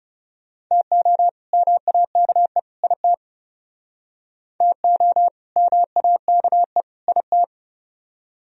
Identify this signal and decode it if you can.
Morse code, heard as a keyed tone, about 38 wpm, 700 Hz. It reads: TO MAKE IT TO MAKE IT